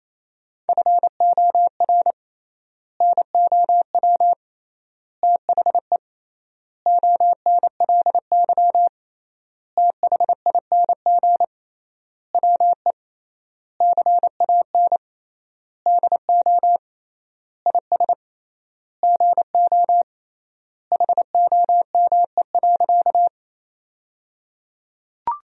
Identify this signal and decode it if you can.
Morse code, heard as a keyed tone, about 28 wpm, 700 Hz. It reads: FOR NOW THE ONLY THING WE CAN DO IS GO HOME.
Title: 3x Word-Spacing - Sentences from Top 200 Words - 28wpm